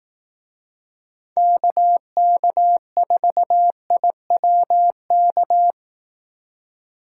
Morse code, heard as a keyed tone, about 18 wpm, 700 Hz. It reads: KK4IWK
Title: Speed Racing - USA 2x3 Call Signs 15wpm